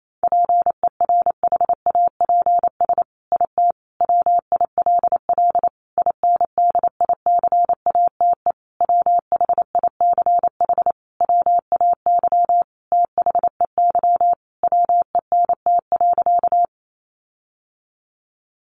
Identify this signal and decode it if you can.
Morse code, heard as a keyed tone, about 28 wpm, 700 Hz. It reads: PERHAPS IT WILL INDICATE WHICH WAY THEY WENT.
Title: Sentences from Top 1300 Words 28wpm